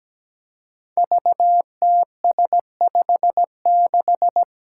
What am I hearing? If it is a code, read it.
Morse code, heard as a keyed tone, about 17 wpm, 700 Hz. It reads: VTS56
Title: Five Challenging Letter-Number Combinations 17wpm